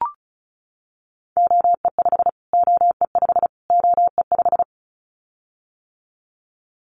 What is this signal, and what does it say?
Morse code, heard as a keyed tone, about 34 wpm, 700 Hz. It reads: OE5 OE5 OE5
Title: Callsign Prefixes for Top 25 DXCC - Repeated 3x - 35wpm